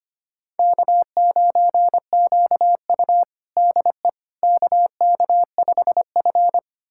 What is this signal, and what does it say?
Morse code, heard as a keyed tone, about 25 wpm, 700 Hz. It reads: K9QU DE KK5F
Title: USA Callsign DE Callsign - 4 Characters - 25wpm